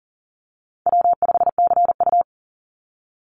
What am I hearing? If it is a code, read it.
Morse code, heard as a keyed tone, about 40 wpm, 700 Hz. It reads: W5CU
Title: Easy Call Signs Repeated 2x - 40wpm